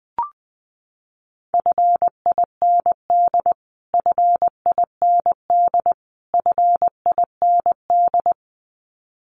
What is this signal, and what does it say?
Morse code, heard as a keyed tone, about 20 wpm, 700 Hz. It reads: FIND FIND FIND
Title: Top 500 Words Repeated 20wpm